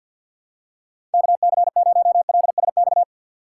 Morse code, heard as a keyed tone, about 47 wpm, 700 Hz. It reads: KC0LSX
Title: USA Call Signs - 6 Characters - 6x Character Spacing - 50wpm